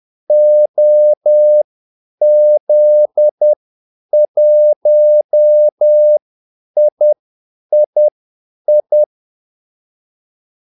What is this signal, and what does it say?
Morse code, heard as a keyed tone, about 10 wpm, spaced as for 8 wpm, 600 Hz. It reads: OZ1III
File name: Callsigns-10-8wpm